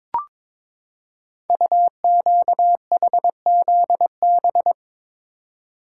Morse code, heard as a keyed tone, about 22 wpm, 700 Hz. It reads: UQHZB
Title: Code Groups - 5 Letters - 22wpm